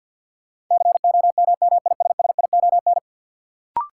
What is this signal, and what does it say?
Morse code, heard as a keyed tone, about 52 wpm, 700 Hz. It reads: COMMISSION
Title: Sentences from Top 1400 Words - New Word Drill Alt - 50wpm